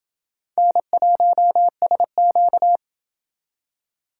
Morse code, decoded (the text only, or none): N1SQ